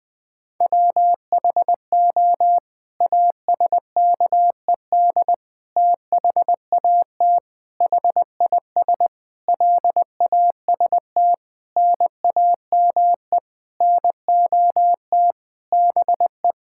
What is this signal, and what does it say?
Morse code, heard as a keyed tone, about 21 wpm, 700 Hz. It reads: WHO ASKED THAT HIS LAST NAME NOT BE
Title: Sets of 9 Words - 20wpm